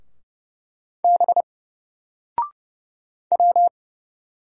Morse code, heard as a keyed tone, about 30 wpm, 700 Hz. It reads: B W
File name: Single-Letter-Number-30wpm